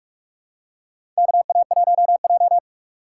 Morse code, heard as a keyed tone, about 45 wpm, 700 Hz. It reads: KA1J
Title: USA Call Signs - 3x Character Spacing - 45wpm